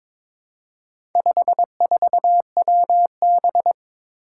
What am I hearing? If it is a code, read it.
Morse code, heard as a keyed tone, about 22 wpm, 700 Hz. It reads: 54WB
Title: Four Challenging Letter-Number Combinations 22wpm